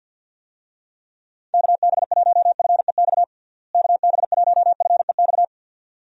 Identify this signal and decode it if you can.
Morse code, heard as a keyed tone, about 53 wpm, 700 Hz. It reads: KB1FEX KB1FEX
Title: USA Call Signs - 6 Characters - Repeated 2x - 50wpm